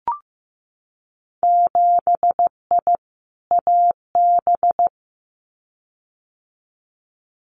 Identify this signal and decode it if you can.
Morse code, heard as a keyed tone, about 15 wpm, 700 Hz. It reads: 7I AB
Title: Winter Field Day Exchanges Encore 15wpm